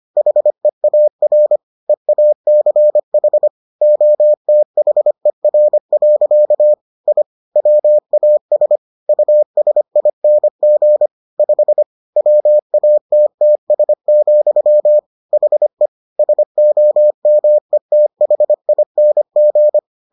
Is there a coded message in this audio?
Yes: Morse code from 0.2 to 19.8 s, heard as a keyed tone, about 25 wpm, 600 Hz. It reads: HEAR EACH OTHER. I WAS USING 5 WATTS, HE SOMETHING